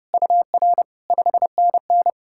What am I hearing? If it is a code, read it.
Morse code, heard as a keyed tone, about 40 wpm, spaced as for 35 wpm, 700 Hz. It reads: UR 5NN
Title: Speed-Racing Common QSO Phrases - 30wpm